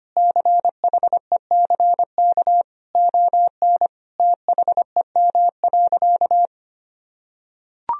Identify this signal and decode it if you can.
Morse code, heard as a keyed tone, about 25 wpm, 700 Hz. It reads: CHECK ON THEM.
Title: Sentences from Top 1000 Words - Review Part 3 - 25wpm